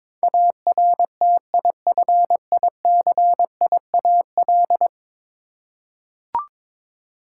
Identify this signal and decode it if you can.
Morse code, heard as a keyed tone, about 22 wpm, 700 Hz. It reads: ARTIFICIAL
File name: practice / 22wpm / 10 Letter Words - First 3 Letters Given - 22wpm